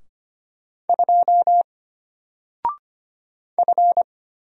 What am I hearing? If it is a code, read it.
Morse code, heard as a keyed tone, about 25 wpm, 700 Hz. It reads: 2 F
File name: Single-Letter-Number-25wpm